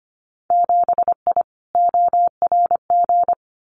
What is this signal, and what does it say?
Morse code, heard as a keyed tone, about 25 wpm, 700 Hz. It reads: 7I ORG
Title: Winter Field Day Exchanges Encore 25wpm